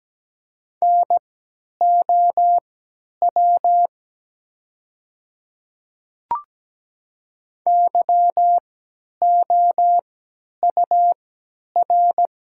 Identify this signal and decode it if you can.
Morse code, heard as a keyed tone, about 17 wpm, spaced as for 10 wpm, 700 Hz. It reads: NOW YOUR
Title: 3x Character Spacing - Top 100 Words - 17wpm